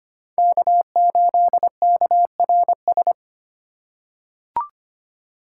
Morse code, heard as a keyed tone, about 25 wpm, 700 Hz. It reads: K8KRS
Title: USA Call Signs - 5 Characters - 4x Character Spacing - 25wpm